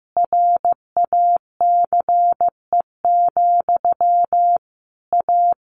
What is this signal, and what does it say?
Morse code, heard as a keyed tone, about 15 wpm, 700 Hz. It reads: RACE, A
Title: A Sound Of Thunder 15wpm